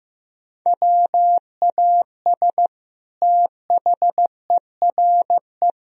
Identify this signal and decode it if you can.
Morse code, heard as a keyed tone, about 16 wpm, 700 Hz. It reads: WAS THERE
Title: Sentences from Top 1000 Words - Review Part 3 - 15wpm